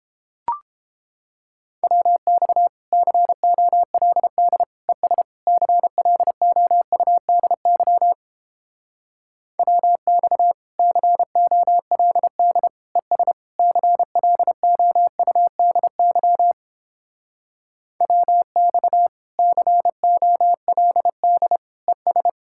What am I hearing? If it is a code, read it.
Morse code, heard as a keyed tone, about 29 wpm, 700 Hz. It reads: WX COLD ES CLOUDY WX COLD ES CLOUDY WX COLD ES